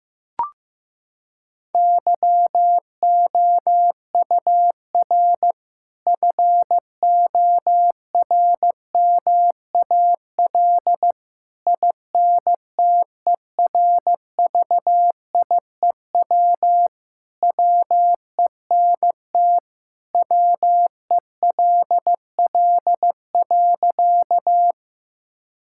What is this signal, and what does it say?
Morse code, heard as a keyed tone, about 15 wpm, 700 Hz. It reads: YOUR FORMAL INTERVIEW WENT WELL.
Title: Sentences from Top 1600 Words 15wpm